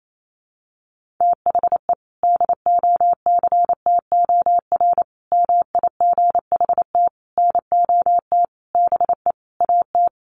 Morse code, heard as a keyed tone, about 28 wpm, 700 Hz. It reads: THE DOCTOR MIGHT NOT BE AT
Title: Sentences from Top 1500 Words 28wpm